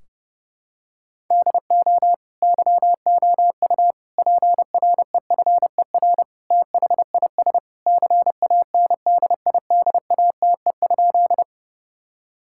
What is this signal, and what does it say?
Morse code, heard as a keyed tone, about 30 wpm, 700 Hz. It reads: DO YOU PREFER THIS CANDIDATE?
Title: Sentences from Top 1600 Words 30wpm